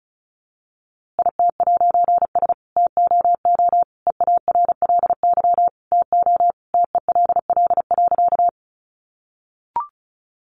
Morse code, heard as a keyed tone, about 34 wpm, 700 Hz. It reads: IT'S TOO EARLY TO TELL.
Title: A Sound Of Thunder 35wpm